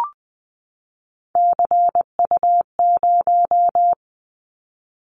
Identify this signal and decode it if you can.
Morse code, heard as a keyed tone, about 20 wpm, 700 Hz. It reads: CU0
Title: Callsign Prefixes for Top 50 DXCC - 20wpm